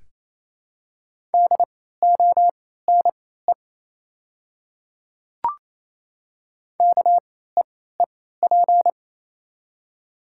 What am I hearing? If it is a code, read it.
Morse code, heard as a keyed tone, about 28 wpm, spaced as for 16 wpm, 700 Hz. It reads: DONE KEEP
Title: 3x Character Spacing - Top 500 Words - 28wpm